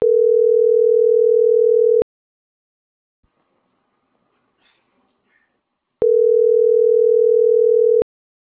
Ringback tone (North America), 440 + 480 Hz.